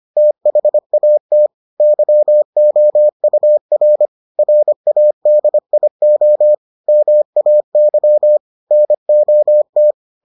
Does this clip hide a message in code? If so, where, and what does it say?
Yes: Morse code from 0.2 to 9.9 s, heard as a keyed tone, about 25 wpm, 600 Hz. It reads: THAT YOUR RADIO MAY NOT